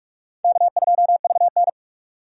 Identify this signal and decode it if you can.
Morse code, heard as a keyed tone, about 46 wpm, 700 Hz. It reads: K2VN
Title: USA Call Signs - 4 Characters - 45wpm